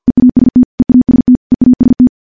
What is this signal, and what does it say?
Morse code, heard as a keyed tone, about 50 wpm, 250 Hz. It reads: EAST EAST EAST
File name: morse / morse-training-50WPM